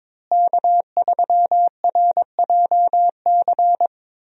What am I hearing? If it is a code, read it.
Morse code, heard as a keyed tone, about 21 wpm, 700 Hz. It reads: K3RJC
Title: USA Call Signs - 5 Characters - Repeated 2x - 22wpm